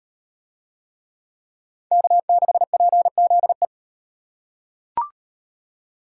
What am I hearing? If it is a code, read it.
Morse code, heard as a keyed tone, about 38 wpm, 700 Hz. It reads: K6PZE